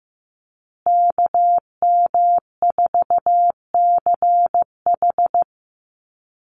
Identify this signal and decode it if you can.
Morse code, heard as a keyed tone, about 15 wpm, 700 Hz. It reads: KM4CH
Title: Call Signs Repeated 2x - 15wpm